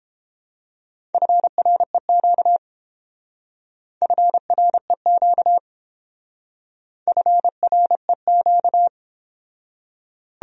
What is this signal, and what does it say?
Morse code, heard as a keyed tone, about 29 wpm, 700 Hz. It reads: FREQ FREQ FREQ